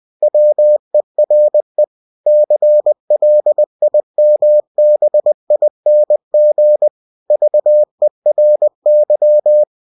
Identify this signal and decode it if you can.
Morse code, heard as a keyed tone, about 20 wpm, 600 Hz. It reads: WERE CLIMBING VERY
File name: winnie-20wpm